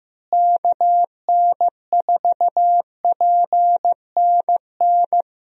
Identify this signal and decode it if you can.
Morse code, heard as a keyed tone, about 15 wpm, 700 Hz. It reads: KN4PNN
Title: USA Call Signs - 6 Characters - Repeated 3x - 15wpm